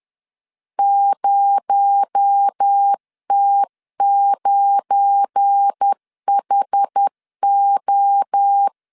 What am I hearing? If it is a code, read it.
Morse code, heard as a keyed tone, about 10 wpm, 800 Hz. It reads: 0T9HO
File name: LEZIONI / 7 GF9Z / 7 GF9 10wpm 2747